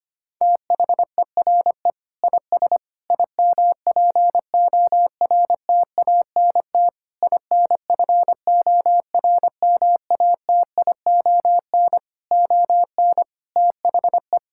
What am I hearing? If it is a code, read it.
Morse code, heard as a keyed tone, about 25 wpm, 700 Hz. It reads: THERE IS IMPORTANT INFORMATION ON THE
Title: Sentences from Top 1000 Words - Review Part 6 - 25wpm